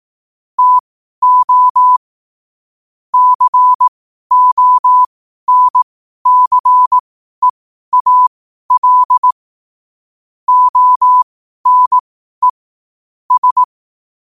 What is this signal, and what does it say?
Morse code, heard as a keyed tone, about 18 wpm, spaced as for 12 wpm, 1000 Hz. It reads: TO CONCEAL ONE S
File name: qotdfiles / 12wpm